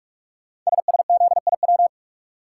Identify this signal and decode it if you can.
Morse code, heard as a keyed tone, about 44 wpm, 690 Hz. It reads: SSZIW
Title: Code Groups - 5 Letters - 45wpm